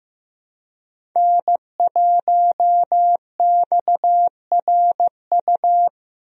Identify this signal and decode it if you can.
Morse code, heard as a keyed tone, about 15 wpm, 700 Hz. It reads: N1XRU
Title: USA Call Signs - 5 Characters - Repeated 2x - 15wpm